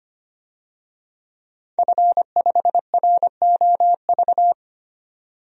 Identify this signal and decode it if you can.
Morse code, heard as a keyed tone, about 25 wpm, 700 Hz. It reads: F5ROV
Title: DXCC-100 Call Signs - 5 Characters - 25wpm